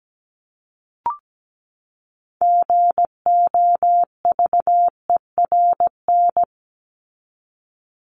Morse code, heard as a keyed tone, about 17 wpm, 700 Hz. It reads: GOVERN
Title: Top 500 Words 17wpm